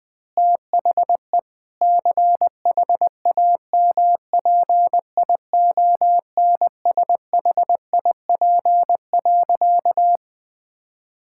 Morse code, heard as a keyed tone, about 20 wpm, 700 Hz. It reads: THE CHAMPIONSHIP.
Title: Sentences with 8-Word Phrases - 20wpm